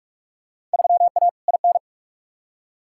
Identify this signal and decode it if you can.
Morse code, heard as a keyed tone, about 45 wpm, 700 Hz. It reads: 3A IN
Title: Field Day Exchanges Encore - 45wpm